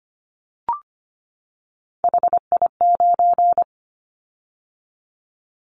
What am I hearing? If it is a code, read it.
Morse code, heard as a keyed tone, about 25 wpm, 700 Hz. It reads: HI9
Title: Callsign Prefixes for Top 100 DXCC - 25wpm